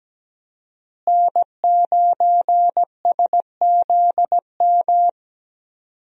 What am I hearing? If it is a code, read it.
Morse code, heard as a keyed tone, about 17 wpm, 700 Hz. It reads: N9SZM
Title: USA Call Signs - 5 Characters - Repeated 3x - 17wpm